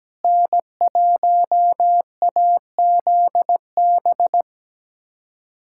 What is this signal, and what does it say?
Morse code, heard as a keyed tone, about 17 wpm, 700 Hz. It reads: N1AZB
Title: USA Callsign DE Callsign - 5 Characters - 17wpm